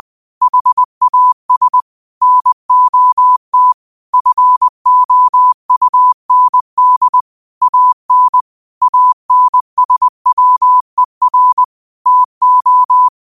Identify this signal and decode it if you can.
Morse code, heard as a keyed tone, about 20 wpm, 1000 Hz. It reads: HAS NOT FOUND AN ANSWER TO
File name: qotdfiles / 20wpm